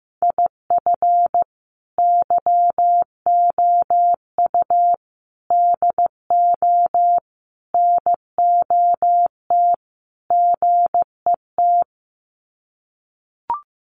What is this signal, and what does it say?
Morse code, heard as a keyed tone, about 15 wpm, 700 Hz. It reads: IF YOU DO NOT GET
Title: Sets of 5 Words Encore2 15wpm